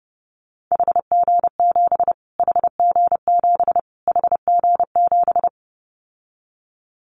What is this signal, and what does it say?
Morse code, heard as a keyed tone, about 30 wpm, 700 Hz. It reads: HG7 HG7 HG7
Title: Callsign Prefixes for Top 25 DXCC - Repeated 3x - 30wpm